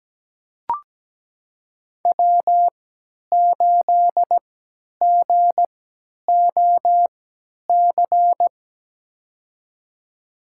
Morse code, heard as a keyed tone, about 17 wpm, spaced as for 10 wpm, 700 Hz. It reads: W8GOC
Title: USA Call Signs - 5 Characters - 3x Character Spacing - 17wpm